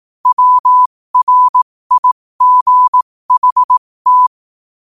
Morse code, heard as a keyed tone, about 18 wpm, spaced as for 15 wpm, 1000 Hz. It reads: WRIGHT
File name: qotdfiles / 15wpm